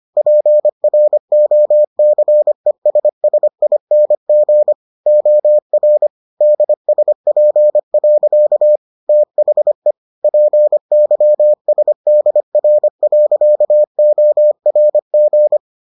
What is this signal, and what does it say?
Morse code, heard as a keyed tone, about 25 wpm, 600 Hz. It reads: PROCESSING OR DSP. THE PYSDR.ORG